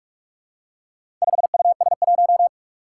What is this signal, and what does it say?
Morse code, heard as a keyed tone, about 46 wpm, 700 Hz. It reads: 5US1
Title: Four Challenging Letter-Number Combinations 45wpm